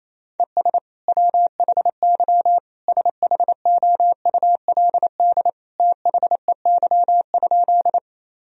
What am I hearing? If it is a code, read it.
Morse code, heard as a keyed tone, about 27 wpm, 700 Hz. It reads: ES WHY SHOULD THEY?
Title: Sentences from Top 300 Words and Abbreviations 28wpm